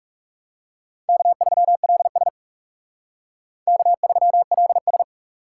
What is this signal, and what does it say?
Morse code, heard as a keyed tone, about 36 wpm, 700 Hz. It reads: K3LS K3LS
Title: Speed Racing - USA 1x2 Call Signs 30wpm